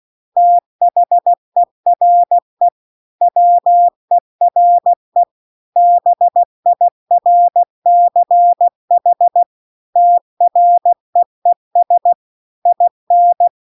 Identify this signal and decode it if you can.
Morse code, heard as a keyed tone, about 16 wpm, 700 Hz. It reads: THERE WERE BIRCH TREES IN